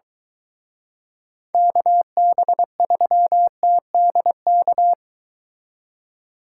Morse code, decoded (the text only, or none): KB3TDK